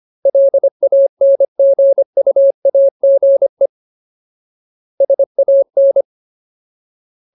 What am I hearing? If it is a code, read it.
Morse code, heard as a keyed tone, about 25 wpm, 550 Hz. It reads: LANGUAGE SAN
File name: Ultimate Word List / 25WPM Ultimate